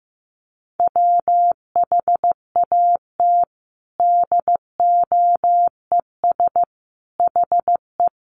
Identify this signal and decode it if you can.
Morse code, heard as a keyed tone, about 15 wpm, 700 Hz. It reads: WHAT DOES HE
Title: Sentences from Top 1300 Words 15wpm